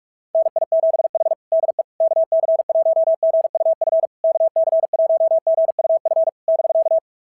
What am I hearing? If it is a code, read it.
Morse code, heard as a keyed tone, about 46 wpm, 690 Hz. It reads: NI7H DE KC1GUF KC1GUF <BK>
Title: SOTA QSO Practice 45wpm